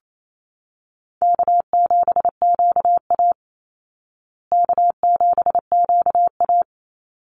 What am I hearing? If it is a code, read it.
Morse code, heard as a keyed tone, about 28 wpm, 700 Hz. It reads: K7QA K7QA
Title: Easy Call Signs Repeated 2x - 28wpm